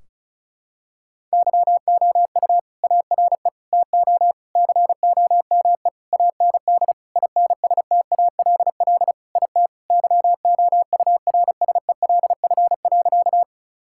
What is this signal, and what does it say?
Morse code, heard as a keyed tone, about 36 wpm, 700 Hz. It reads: YOU ARE TO COME AND INSTALL IT YOURSELF.
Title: Sentences from Top 1600 Words 35wpm